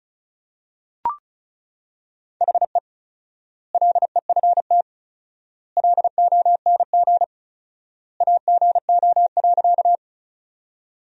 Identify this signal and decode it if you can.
Morse code, heard as a keyed tone, about 34 wpm, 700 Hz. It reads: HE LEFT LONG AGO.
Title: 4x Word-Spacing - Sentences from Top 500 Words - 35wpm